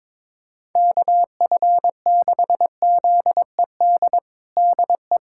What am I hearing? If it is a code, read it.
Morse code, heard as a keyed tone, about 22 wpm, 700 Hz. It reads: KF6ZED DE
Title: USA Callsign DE Callsign - 6 Characters - 22wpm